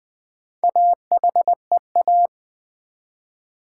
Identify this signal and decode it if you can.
Morse code, heard as a keyed tone, about 20 wpm, 700 Hz. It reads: AHEA
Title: Build A Word - 1 Letter At A Time - 5-Letter Words - 20wpm